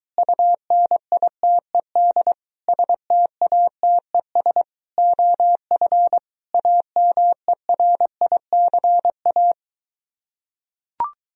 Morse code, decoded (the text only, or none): UNITED STATES OF AMERICA